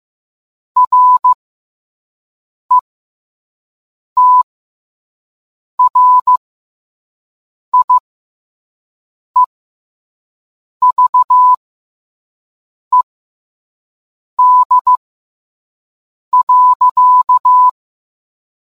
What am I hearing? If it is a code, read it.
Morse code, heard as a keyed tone, about 15 wpm, spaced as for 5 wpm, 1000 Hz. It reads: RETRIEVED.